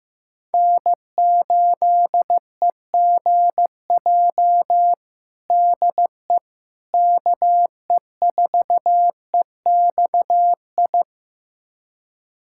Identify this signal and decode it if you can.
Morse code, heard as a keyed tone, about 15 wpm, 700 Hz. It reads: N8EGJ DE KE4EXI
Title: USA Callsign DE Callsign 15wpm